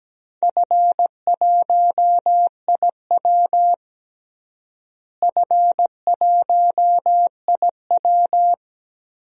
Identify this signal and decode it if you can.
Morse code, heard as a keyed tone, about 17 wpm, 700 Hz. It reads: F1IW F1IW
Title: DXCC-100 Call Signs - 4 Characters - Repeated 3x - 17wpm